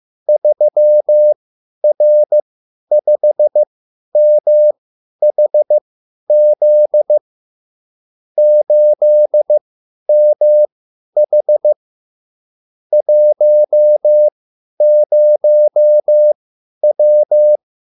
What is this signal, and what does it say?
Morse code, heard as a keyed tone, about 15 wpm, spaced as for 10 wpm, 600 Hz. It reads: 3R5MHZ 8MH 10W